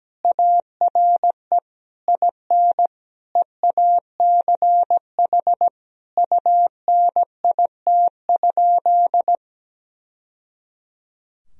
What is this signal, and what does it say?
Morse code, heard as a keyed tone, about 17 wpm, 700 Hz. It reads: ARE IN EACH UNIT?
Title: Sentences from Top 1000 Words - Review Part 2 - 17wpm